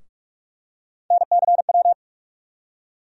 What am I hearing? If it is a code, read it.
Morse code, heard as a keyed tone, about 47 wpm, 700 Hz. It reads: NCW
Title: USA Call Sign Postfixes - 3 Letters 45wpm